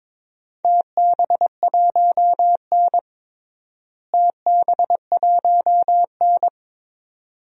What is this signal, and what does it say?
Morse code, heard as a keyed tone, about 22 wpm, 700 Hz. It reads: TB1N TB1N
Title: DXCC-100 Call Signs - 4 Characters - Repeated 2x - 22wpm